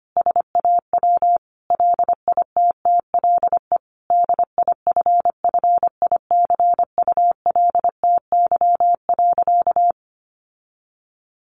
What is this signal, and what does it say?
Morse code, heard as a keyed tone, about 25 wpm, 700 Hz. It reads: SAW LITTLE DIFFICULTY.
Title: Sentences from Top 1500 Words 25wpm